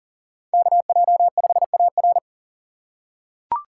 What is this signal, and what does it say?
Morse code, heard as a keyed tone, about 40 wpm, 700 Hz. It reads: KJ5AR
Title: USA Call Signs - 5 Characters - 2x Character Spacing - 40wpm